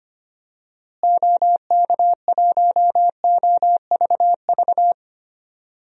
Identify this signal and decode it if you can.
Morse code, heard as a keyed tone, about 25 wpm, 700 Hz. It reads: OK1OVV
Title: DXCC-100 Call Signs - 6 Characters - Repeated 2x - 25wpm